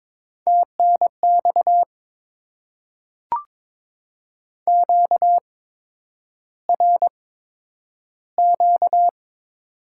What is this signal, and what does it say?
Morse code, heard as a keyed tone, about 22 wpm, spaced as for 9 wpm, 700 Hz. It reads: TNX QRQ
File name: Top 90 QSO Elements - 8x Character Spacing - 22wpm